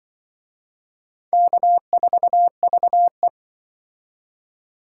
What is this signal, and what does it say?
Morse code, heard as a keyed tone, about 25 wpm, 700 Hz. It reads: K4VE
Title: Speed Racing - USA 1x2 Call Signs 20wpm